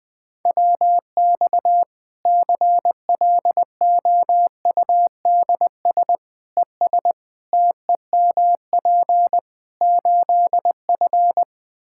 Morse code, heard as a keyed tone, about 21 wpm, 700 Hz. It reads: WX CLOUDS ES TEMP 8F
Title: Practice QSO - Exchange 2 - Info - 20wpm